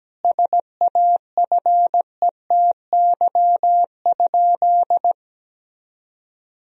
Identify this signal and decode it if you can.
Morse code, heard as a keyed tone, about 17 wpm, 700 Hz. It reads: SAFETY?
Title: Sentences from Top 1000 Words - Review Part 4 - 17wpm